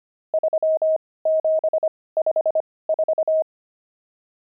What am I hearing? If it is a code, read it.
Morse code, heard as a keyed tone, about 25 wpm, spaced as for 18 wpm, 630 Hz. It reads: 3754
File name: pangram-fs-18-cs-25-nums6